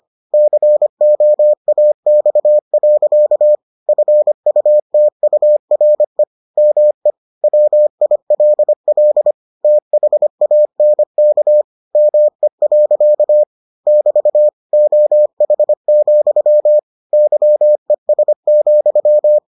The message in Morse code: COAX. FUTURE ME WILL THANK ME. <BT> OH, YES,

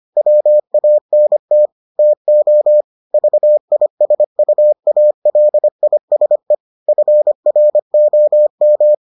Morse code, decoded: WANT TO VISUALISE FROM

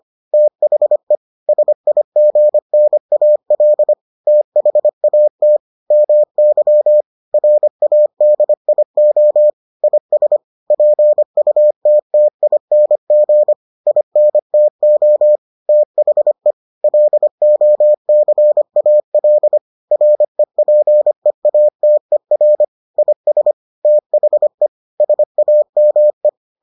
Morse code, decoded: THE SIGNAL THAT MY RADIO IS PUTTING INTO THE LOCAL REPEATER IS THE SAME

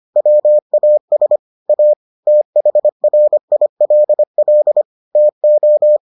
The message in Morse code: WAS A THRILL TO

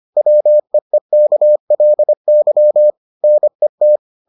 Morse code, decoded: WEEKLY NET

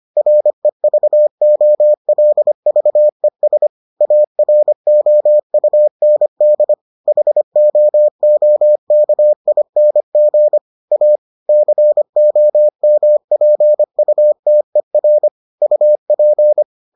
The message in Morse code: REVOLVES AROUND HOOKING A COMPUTER UP

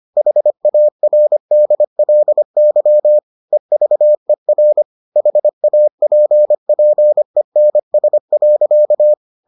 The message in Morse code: HARDLY EVER HAPPENS.

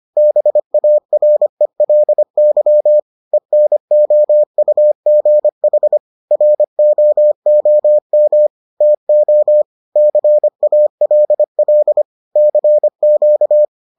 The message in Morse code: BARELY ENOUGH ROOM TO CALL CQ

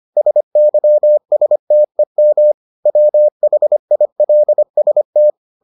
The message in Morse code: SYSTEM WHILST

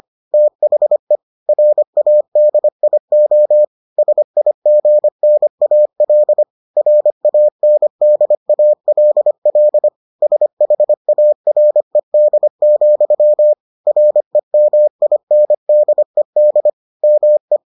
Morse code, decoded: THE RADIO SIGNAL RANDALL SHARED, REMINDED ME

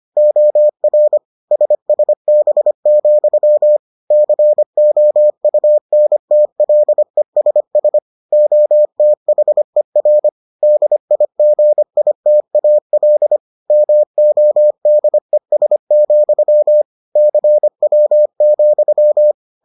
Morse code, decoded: OR SSB, COUNTLESS OTHER DIGITAL MODES, CW,

